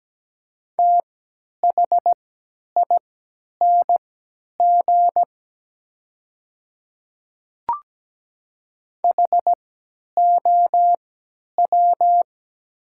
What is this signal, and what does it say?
Morse code, heard as a keyed tone, about 17 wpm, spaced as for 10 wpm, 700 Hz. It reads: THING HOW
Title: 3x Character Spacing - Top 300 Words - 17wpm